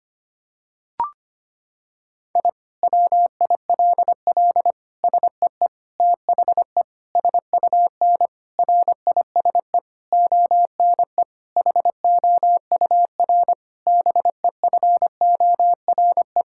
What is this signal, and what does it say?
Morse code, heard as a keyed tone, about 25 wpm, 700 Hz. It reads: I WILL SEE THE SUN RISE ONE HOUR BEFORE
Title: Sentences from Top 1000 Words - Review Part 6 - 25wpm